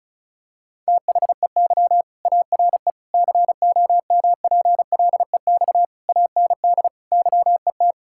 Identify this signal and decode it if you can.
Morse code, heard as a keyed tone, about 36 wpm, 700 Hz. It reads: THEY ARE COMPLEX AND YET
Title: Sentences from Top 1000 Words - Review Part 6 - 35wpm